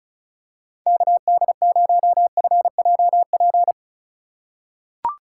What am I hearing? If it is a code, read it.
Morse code, heard as a keyed tone, about 36 wpm, 690 Hz. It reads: KD0FJP
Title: USA Call Signs - 6 Characters - 4x Character Spacing - 35wpm